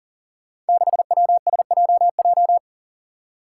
Morse code, heard as a keyed tone, about 40 wpm, 700 Hz. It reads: 6WSJJ
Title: Five Challenging Letter-Number Combinations 40wpm